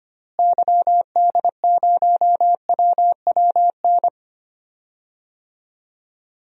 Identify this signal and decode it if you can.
Morse code, heard as a keyed tone, about 25 wpm, 700 Hz. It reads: YD0WWN
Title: DXCC-100 Call Signs - 6 Characters - Repeated 3x - 25wpm